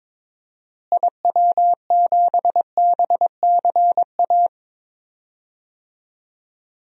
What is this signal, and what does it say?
Morse code, heard as a keyed tone, about 22 wpm, 700 Hz. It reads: IW7BCA
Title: DXCC-100 Call Signs - 6 Characters - Repeated 2x - 22wpm